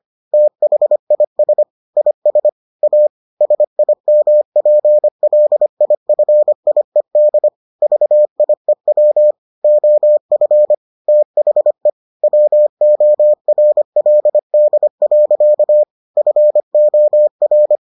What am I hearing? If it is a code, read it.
Morse code, heard as a keyed tone, about 25 wpm, 600 Hz. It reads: THIS IS A SIMPLIFIED VIEW OF THE WORLD. FOR